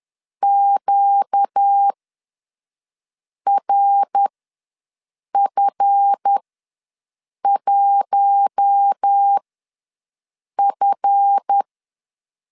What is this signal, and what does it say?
Morse code, heard as a keyed tone, about 10 wpm, spaced as for 5 wpm, 800 Hz. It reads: QRF1F
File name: LEZIONI / 7 GF9Z / 7 GF ws 10wpm 3918